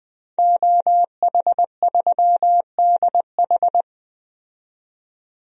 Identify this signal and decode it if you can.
Morse code, heard as a keyed tone, about 20 wpm, 700 Hz. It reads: OH3DH
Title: DXCC-100 Call Signs - 5 Characters - 20wpm